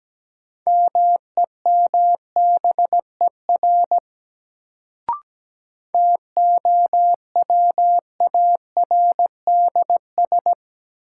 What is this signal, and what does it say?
Morse code, heard as a keyed tone, about 17 wpm, 700 Hz. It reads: MEMBER TOWARDS
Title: Sentences from Top 1100 Words - New Word Drill Alt - 17wpm